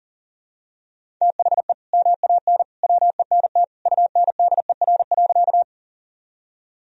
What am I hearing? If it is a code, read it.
Morse code, heard as a keyed tone, about 37 wpm, 700 Hz. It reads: THE MAN WENT UNDER.
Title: Sentences from Top 1000 Words - Review Part 1 - 40wpm